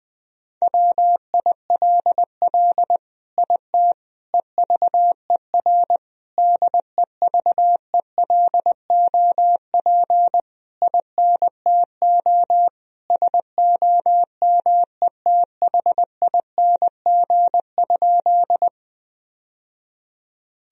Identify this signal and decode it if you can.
Morse code, heard as a keyed tone, about 20 wpm, 700 Hz. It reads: WILL IT EVER DEVELOP INTO SOMETHING?
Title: Sentences from Top 1000 Words - Review Part 2 - 20wpm